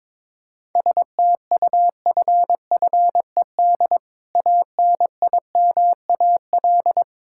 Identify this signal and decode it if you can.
Morse code, heard as a keyed tone, about 22 wpm, 700 Hz. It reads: STUFFED ANIMAL
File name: practice / 22wpm / Difficult Words with Hints - 22wpm